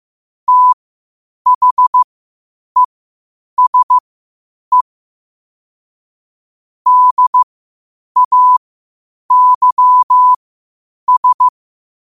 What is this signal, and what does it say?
Morse code, heard as a keyed tone, about 15 wpm, spaced as for 8 wpm, 1000 Hz. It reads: THESE DAYS
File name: qotdfiles / 8wpm